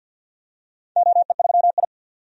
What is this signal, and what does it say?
Morse code, heard as a keyed tone, about 51 wpm, 700 Hz. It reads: OE3I
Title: DXCC-100 Call Signs - 4 Characters - 50wpm